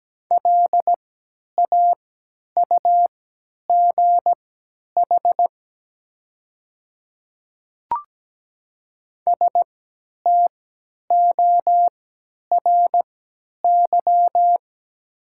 Morse code, heard as a keyed tone, about 17 wpm, spaced as for 10 wpm, 700 Hz. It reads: LAUGH STORY
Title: 3x Character Spacing - Top 500 Words - 17wpm